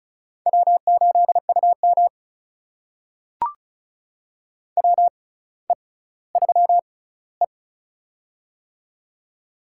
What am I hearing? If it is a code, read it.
Morse code, heard as a keyed tone, about 34 wpm, spaced as for 18 wpm, 700 Hz. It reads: W8UM WE3E
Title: USA Call Signs - 4 Characters - 6x Character Spacing - 35wpm